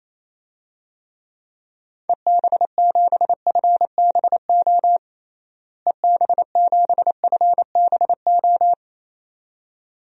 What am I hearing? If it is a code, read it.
Morse code, heard as a keyed tone, about 27 wpm, 700 Hz. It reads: EB7FBO EB7FBO
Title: DXCC-100 Call Signs - 6 Characters - Repeated 2x - 28wpm